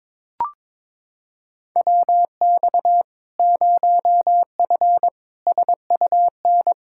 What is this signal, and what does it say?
Morse code, heard as a keyed tone, about 22 wpm, 700 Hz. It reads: WX 0F SUN
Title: Practice QSO - Exchange 2 - Info - 22wpm